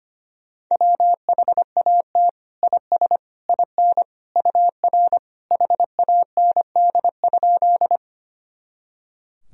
Morse code, heard as a keyed tone, about 25 wpm, 700 Hz. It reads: WHAT IS IN UR HAND?
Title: Sentences from Top 200 Words and Abbreviations 25wpm